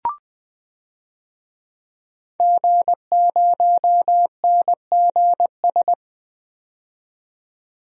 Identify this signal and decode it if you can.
Morse code, heard as a keyed tone, about 20 wpm, 700 Hz. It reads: G0NGS
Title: DXCC-100 Call Signs - 5 Characters - 20wpm